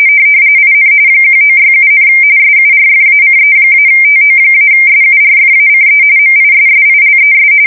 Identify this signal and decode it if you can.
RTTY, about 45.45 baud. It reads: IN 1967, SURFACE-TO-SURFACE MISSILES
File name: fsk170 / USS-New-Jersey